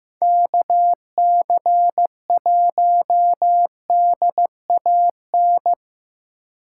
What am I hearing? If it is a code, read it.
Morse code, heard as a keyed tone, about 18 wpm, 700 Hz. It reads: KC1DAN
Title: Speed Racing - USA Call Signs 15wpm